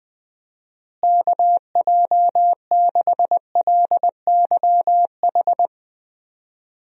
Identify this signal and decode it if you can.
Morse code, heard as a keyed tone, about 20 wpm, 700 Hz. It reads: KJ6LYH
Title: USA Call Signs - 6 Characters - Repeated 2x - 20wpm